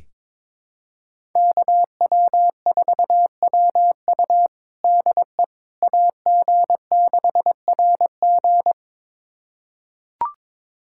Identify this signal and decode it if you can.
Morse code, heard as a keyed tone, about 22 wpm, 700 Hz. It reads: KW4WU DE AG6RG
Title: USA Callsign DE Callsign - 5 Characters - 22wpm